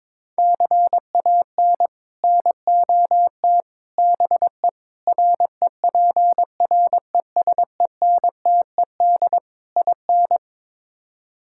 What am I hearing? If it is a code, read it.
Morse code, heard as a keyed tone, about 22 wpm, 700 Hz. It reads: CAN NOT BE REPRESENTED IN
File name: practice / 22wpm / Build A Phrase - 5-Word Phrases - 22wpm